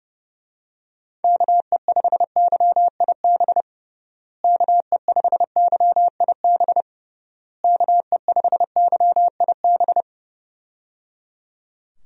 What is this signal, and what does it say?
Morse code, heard as a keyed tone, about 29 wpm, 700 Hz. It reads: KE5YIB KE5YIB KE5YIB
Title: USA Call Signs - 6 Characters - Repeated 3x - 30wpm